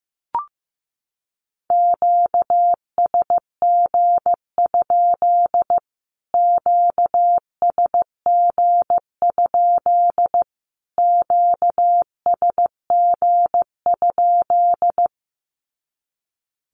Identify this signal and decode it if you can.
Morse code, heard as a keyed tone, about 15 wpm, 700 Hz. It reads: QSG? QSG? QSG?
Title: Q-Codes Repeated 3x 15wpm